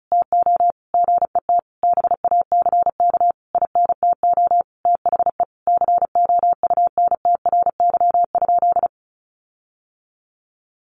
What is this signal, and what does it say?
Morse code, heard as a keyed tone, about 34 wpm, 700 Hz. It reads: TO GET BACK INTO THE COUNTRY?
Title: Sentences from Top 1400 Words 35wpm